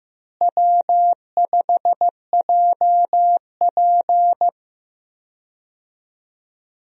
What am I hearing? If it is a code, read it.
Morse code, heard as a keyed tone, about 15 wpm, 700 Hz. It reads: W5JP
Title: USA Call Signs - 4 Characters - Repeated 3x - 15wpm